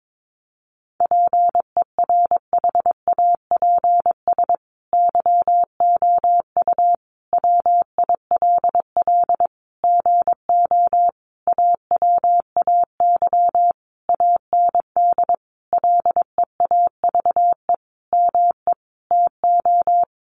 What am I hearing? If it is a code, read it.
Morse code, heard as a keyed tone, about 22 wpm, 700 Hz. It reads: PERHAPS YOU WILL GO AWAY AND LEAVE ME TO
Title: Sentences from Top 1300 Words 22wpm